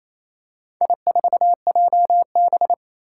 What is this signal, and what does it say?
Morse code, heard as a keyed tone, about 27 wpm, 700 Hz. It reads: I4JB
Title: Four Challenging Letter-Number Combinations 28wpm